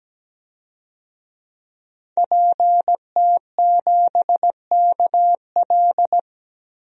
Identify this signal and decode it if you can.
Morse code, heard as a keyed tone, about 17 wpm, 700 Hz. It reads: PT7KL
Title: DXCC-100 Call Signs - 5 Characters - 17wpm